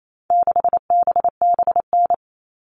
Morse code, heard as a keyed tone, about 28 wpm, 700 Hz. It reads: 6BBN
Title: All Cut Numbers - 4-Digit 28wpm